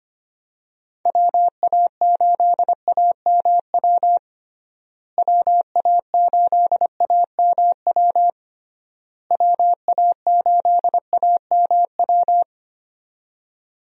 Morse code, heard as a keyed tone, about 25 wpm, 700 Hz. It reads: WA8AMW WA8AMW WA8AMW
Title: USA Call Signs - 6 Characters - Repeated 3x - 25wpm